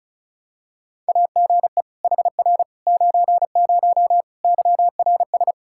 Morse code, heard as a keyed tone, about 34 wpm, 700 Hz. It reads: AGE HR 90 YRS
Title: K1USN SST QSO Practice 35wpm